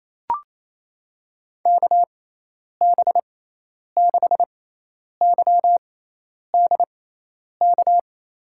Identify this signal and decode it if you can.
Morse code, heard as a keyed tone, about 28 wpm, spaced as for 10 wpm, 700 Hz. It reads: KB6YDK
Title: USA Call Signs - 6 Characters - 6x Character Spacing - 28wpm